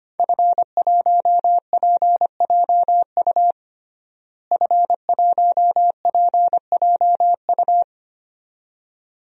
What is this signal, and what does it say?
Morse code, heard as a keyed tone, about 25 wpm, 700 Hz. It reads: F1PJU F1PJU
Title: DXCC-100 Call Signs - 5 Characters - Repeated 2x - 25wpm